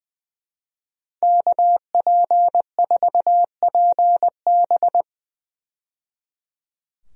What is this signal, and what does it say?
Morse code, heard as a keyed tone, about 20 wpm, 700 Hz. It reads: KP4PB
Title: DXCC-100 Call Signs - 5 Characters - 20wpm